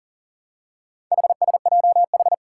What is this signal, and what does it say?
Morse code, heard as a keyed tone, about 43 wpm, 700 Hz. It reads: HSJH
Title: Four Challenging Letter-Number Combinations 40wpm